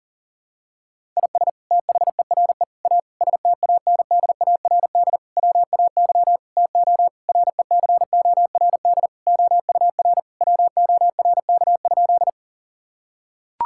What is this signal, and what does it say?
Morse code, heard as a keyed tone, about 40 wpm, 700 Hz. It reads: IS THERE A STANDARD WAY TO RECORD OUR WORK?
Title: Sentences from Top 1000 Words - Review Part 6 - 40wpm